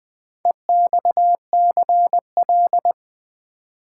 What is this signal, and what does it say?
Morse code, heard as a keyed tone, about 20 wpm, 700 Hz. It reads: EXCL
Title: Build A Word - 4 Letters At A Time - 11-Letter Words - 20wpm